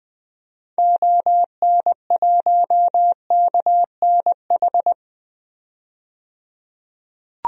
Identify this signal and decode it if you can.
Morse code, heard as a keyed tone, about 20 wpm, 700 Hz. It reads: ON1KNH
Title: DXCC-100 Call Signs - 6 Characters - 20wpm